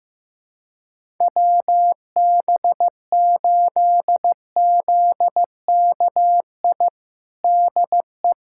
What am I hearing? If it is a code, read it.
Morse code, heard as a keyed tone, about 15 wpm, 700 Hz. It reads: WB8ZKI DE
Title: USA Callsign DE Callsign - 6 Characters - 15wpm